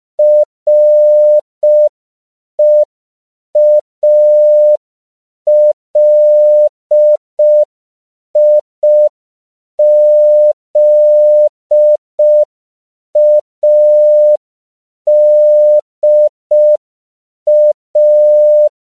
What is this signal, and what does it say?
Morse code, heard as a keyed tone, about 5 wpm, 600 Hz. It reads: REALIZADA
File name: CW / 5PPM